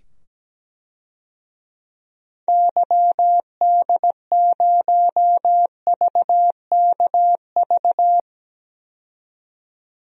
Morse code, heard as a keyed tone, about 17 wpm, 700 Hz. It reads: YD0VKV
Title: DXCC-100 Call Signs - 6 Characters - 17wpm